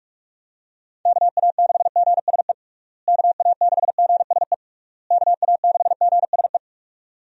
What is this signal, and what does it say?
Morse code, heard as a keyed tone, about 46 wpm, 700 Hz. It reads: KA6GSE KA6GSE KA6GSE
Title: USA Call Signs - 6 Characters - Repeated 3x - 45wpm